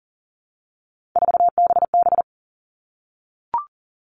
Morse code, heard as a keyed tone, about 40 wpm, 700 Hz. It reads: 4BB
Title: All Cut Numbers - 3-Digit 40wpm